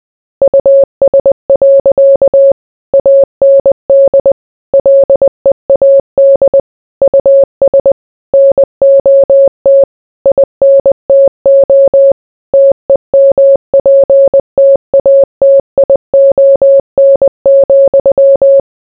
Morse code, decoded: US. AND LEAD US NOT INTO TEMPTATION,